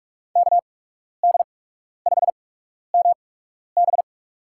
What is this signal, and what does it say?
Morse code, heard as a keyed tone, about 47 wpm, spaced as for 12 wpm, 700 Hz. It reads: KD5MB
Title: USA Call Signs - 5 Characters - 8x Character Spacing - 45wpm